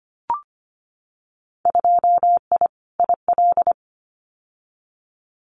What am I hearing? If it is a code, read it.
Morse code, heard as a keyed tone, about 25 wpm, 700 Hz. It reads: 2I IL
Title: Winter Field Day Exchanges 25wpm